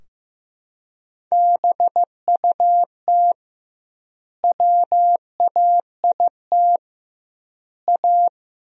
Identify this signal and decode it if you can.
Morse code, heard as a keyed tone, about 15 wpm, 700 Hz. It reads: BUT WAIT A
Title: 2x Word-Spacing - Sentences from Top 500 Words - 15wpm